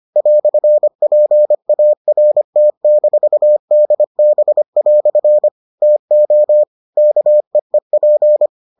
Morse code, heard as a keyed tone, about 25 wpm, 600 Hz. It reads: "PART-DB" TO KEEP